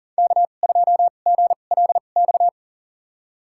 Morse code, heard as a keyed tone, about 40 wpm, spaced as for 29 wpm, 700 Hz. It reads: K2GLX
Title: USA Call Signs - 5 Characters - 2x Character Spacing - 40wpm